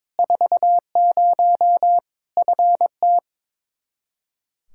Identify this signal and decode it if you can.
Morse code, heard as a keyed tone, about 22 wpm, 700 Hz. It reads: 40 FT
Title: K1USN SST QSO Practice 22wpm